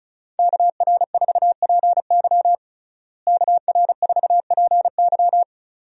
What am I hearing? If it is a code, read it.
Morse code, heard as a keyed tone, about 36 wpm, 700 Hz. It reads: KR4PY KR4PY
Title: USA Call Signs - 5 Characters - Repeated 2x - 35wpm